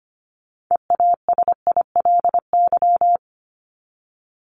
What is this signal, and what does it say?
Morse code, heard as a keyed tone, about 25 wpm, 700 Hz. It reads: EASILY
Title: LY Suffix - Top 100 Words with Base Words 25wpm